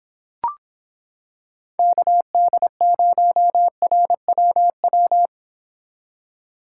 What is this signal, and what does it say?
Morse code, heard as a keyed tone, about 26 wpm, 700 Hz. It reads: KD0RWW